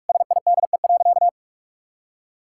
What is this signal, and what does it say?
Morse code, heard as a keyed tone, about 41 wpm, 700 Hz. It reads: SIDE.
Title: Sentences from Top 1000 Words - Review Part 6 - 45wpm